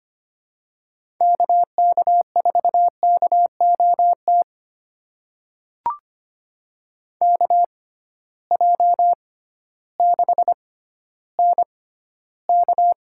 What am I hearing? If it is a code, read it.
Morse code, heard as a keyed tone, about 25 wpm, spaced as for 13 wpm, 700 Hz. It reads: KK4KOT KJ6NK
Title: USA Call Signs - 6 Characters - 6x Character Spacing - 25wpm